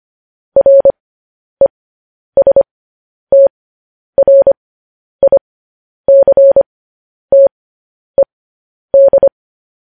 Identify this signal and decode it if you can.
Morse code, heard as a keyed tone, about 25 wpm, spaced as for 10 wpm, 550 Hz. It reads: RESTRICTED
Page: news headlines in morse code at 10 wpm